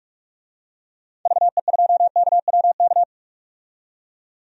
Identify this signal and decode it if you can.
Morse code, heard as a keyed tone, about 45 wpm, 700 Hz. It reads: VE2KWK
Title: DXCC-100 Call Signs - 6 Characters - 45wpm